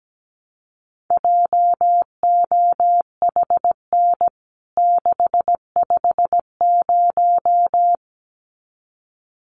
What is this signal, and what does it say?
Morse code, heard as a keyed tone, about 17 wpm, 700 Hz. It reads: JOHN 650
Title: CWT Exchange 17wpm